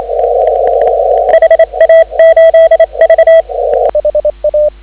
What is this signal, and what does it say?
Morse code, heard as a keyed tone, about 27 wpm, 3250 Hz. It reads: HA8V
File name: CW / 2017 HA-OB HA5YG